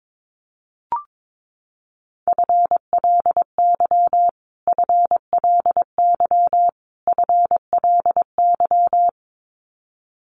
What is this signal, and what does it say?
Morse code, heard as a keyed tone, about 22 wpm, 700 Hz. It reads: FLY FLY FLY
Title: Top 500 Words Repeated 22wpm